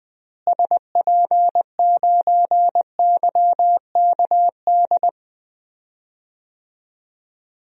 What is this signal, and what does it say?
Morse code, heard as a keyed tone, about 19 wpm, 700 Hz. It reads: SP9YKD
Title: DXCC-100 Call Signs - 6 Characters - Repeated 2x - 20wpm